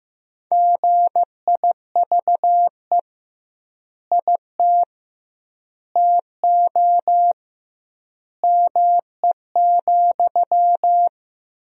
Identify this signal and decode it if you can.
Morse code, heard as a keyed tone, about 15 wpm, 700 Hz. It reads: GIVE IT TO ME,
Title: 2x Word-Spacing - Sentences from Top 200 Words - 15wpm